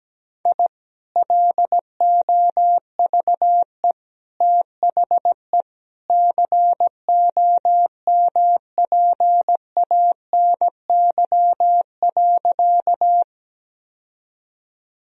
Morse code, heard as a keyed tone, about 17 wpm, 700 Hz. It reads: I LOVE THE COMPANY.
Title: Sentences from Top 1000 Words - Review Part 3 - 17wpm